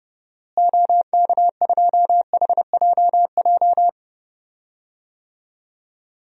Morse code, heard as a keyed tone, about 30 wpm, 700 Hz. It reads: OK2HJJ
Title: DXCC-100 Call Signs - 6 Characters - Repeated 2x - 30wpm